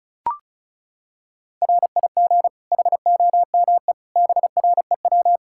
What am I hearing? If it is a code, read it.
Morse code, heard as a keyed tone, about 34 wpm, 700 Hz. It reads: RIG HOME BREW
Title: Practice QSO - Exchange 2 - Info - 35wpm